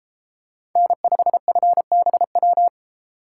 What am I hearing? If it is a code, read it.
Morse code, heard as a keyed tone, about 33 wpm, 700 Hz. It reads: N5FBW